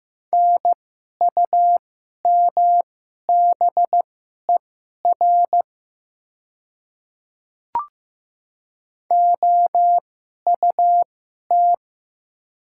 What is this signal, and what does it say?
Morse code, heard as a keyed tone, about 15 wpm, spaced as for 11 wpm, 700 Hz. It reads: NUMBER OUT
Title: 2x Character Spacing - Top 100 Words - 15wpm